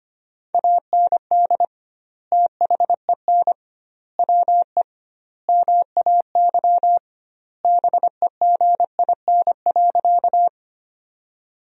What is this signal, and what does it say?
Morse code, heard as a keyed tone, about 25 wpm, 700 Hz. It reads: AND THEN WE MAY BEGIN.
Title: 2x Word-Spacing - Sentences from Top 300 Words - 25wpm